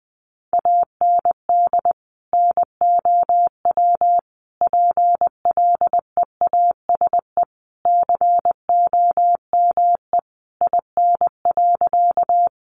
Morse code, heard as a keyed tone, about 20 wpm, 700 Hz. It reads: AND NOW PLEASE COME IN.
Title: Sentences from Top 1500 Words 20wpm